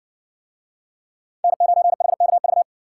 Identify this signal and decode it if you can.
Morse code, heard as a keyed tone, about 60 wpm, 700 Hz. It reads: N9HKV